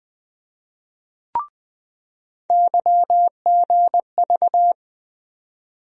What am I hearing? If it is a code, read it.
Morse code, heard as a keyed tone, about 20 wpm, 700 Hz. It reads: YGV
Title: USA Call Sign Postfixes - 3 Letters 20wpm